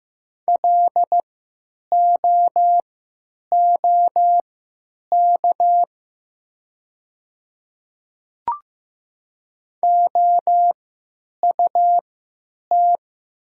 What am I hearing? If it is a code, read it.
Morse code, heard as a keyed tone, about 15 wpm, spaced as for 9 wpm, 700 Hz. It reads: LOOK OUT
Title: 3x Character Spacing - Top 100 Words - 15wpm